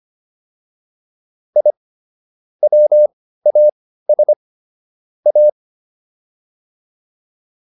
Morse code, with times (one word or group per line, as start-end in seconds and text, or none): 1.6-1.7 I
2.6-4.3 WAS
5.3-5.5 A